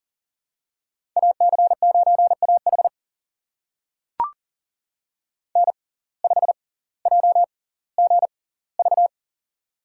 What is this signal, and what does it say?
Morse code, heard as a keyed tone, about 40 wpm, spaced as for 20 wpm, 700 Hz. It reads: AC9AH N5JGV
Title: USA Call Signs - 5 Characters - 6x Character Spacing - 40wpm